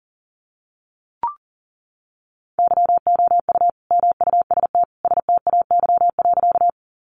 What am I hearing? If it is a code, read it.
Morse code, heard as a keyed tone, about 40 wpm, 700 Hz. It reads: YOU MUST STAY.
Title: Sentences from Top 500 Words 40wpm